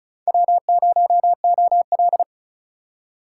Morse code, heard as a keyed tone, about 47 wpm, spaced as for 42 wpm, 700 Hz. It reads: W0OL
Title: Speed Racing - USA 1x2 Call Signs 35wpm